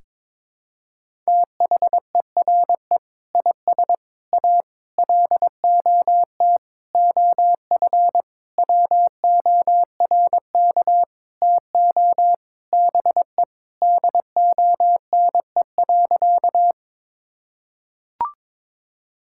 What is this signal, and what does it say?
Morse code, heard as a keyed tone, about 22 wpm, 700 Hz. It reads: THERE IS A LOT OF WORK TO BE DONE.
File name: Sentences with 7-Word Phrases - 22wpm